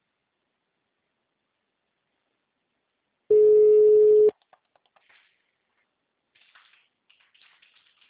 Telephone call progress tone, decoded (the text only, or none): ringback tone